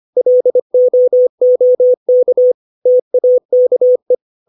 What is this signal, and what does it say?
Morse code, heard as a keyed tone, about 25 wpm, 500 Hz. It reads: LOOK TAKE
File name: 632 words / 632 Words 25WPM